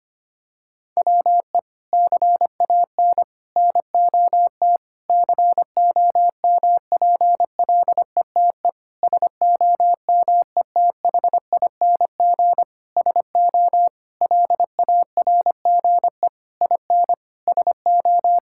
Morse code, heard as a keyed tone, about 25 wpm, 700 Hz. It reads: WE CAN NOT COMPLETE SOMETHING SO LARGE IN SO
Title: Sentences from Top 1000 Words - Review Part 6 - 25wpm